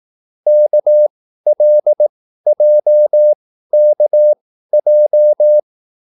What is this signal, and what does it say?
Morse code, heard as a keyed tone, about 18 wpm, spaced as for 13 wpm, 600 Hz. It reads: KLJKJ